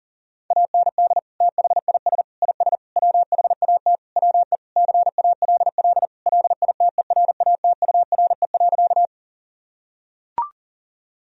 Morse code, heard as a keyed tone, about 40 wpm, 700 Hz. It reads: AND THIS IS WHAT WE CALL LITERATURE.
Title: Sentences from Top 1000 Words - Review Part 3 - 40wpm